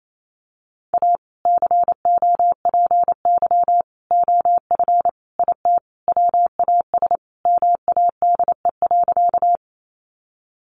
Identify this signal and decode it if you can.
Morse code, heard as a keyed tone, about 28 wpm, 700 Hz. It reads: A COPY OF IT WAS MADE.
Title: Sentences from Top 1400 Words 28wpm